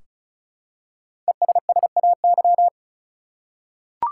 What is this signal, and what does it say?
Morse code, heard as a keyed tone, about 36 wpm, 700 Hz. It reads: ESSAY
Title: Five Letter Words - 35wpm